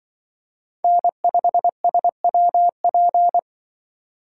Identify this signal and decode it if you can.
Morse code, heard as a keyed tone, about 24 wpm, 700 Hz. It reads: N5SWP
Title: Speed Racing - USA 1x3 Call Signs 20wpm